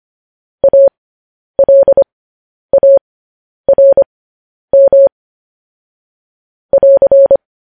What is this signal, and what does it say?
Morse code, heard as a keyed tone, about 25 wpm, spaced as for 10 wpm, 550 Hz. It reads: ALARM <AR>
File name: podcast / News headlines-1x-25x10